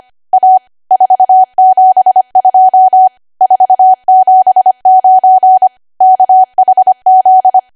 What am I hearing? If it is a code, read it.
Morse code, heard as a keyed tone, about 25 wpm, 750 Hz. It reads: A 472 479 KHZ
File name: cwa / W1AW-Level3 / 25WPM